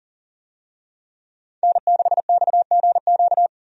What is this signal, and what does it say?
Morse code, heard as a keyed tone, about 40 wpm, 700 Hz. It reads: N6XGQ
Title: USA Call Signs - 5 Characters - Repeated 3x - 40wpm